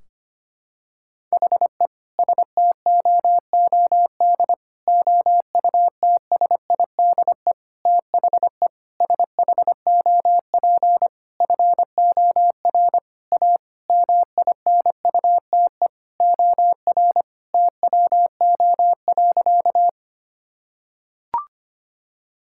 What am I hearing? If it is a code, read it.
Morse code, heard as a keyed tone, about 25 wpm, 700 Hz. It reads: HE STOOD OUTSIDE THE SHOP FOR A MINUTE OR TWO.
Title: Sentences from Top 1600 Words 25wpm